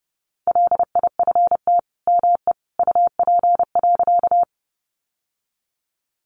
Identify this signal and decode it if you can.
Morse code, heard as a keyed tone, about 30 wpm, 700 Hz. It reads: LIFT ME UP.
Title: Sentences from Top 1000 Words 30wpm